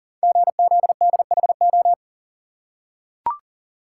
Morse code, heard as a keyed tone, about 38 wpm, 690 Hz. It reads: GZDHO
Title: Code Groups - 5 Letters - 40wpm